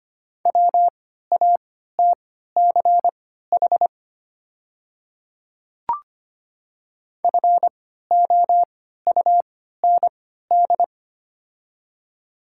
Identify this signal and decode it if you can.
Morse code, heard as a keyed tone, about 25 wpm, spaced as for 14 wpm, 700 Hz. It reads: WATCH FOUND
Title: 3x Character Spacing - Top 500 Words - 25wpm